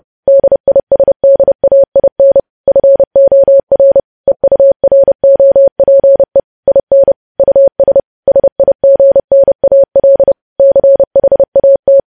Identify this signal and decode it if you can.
Morse code, heard as a keyed tone, about 30 wpm, 550 Hz. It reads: DISDAIN FOR EUROPE IN US SIGNAL CHAT
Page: news headlines in morse code at 30 wpm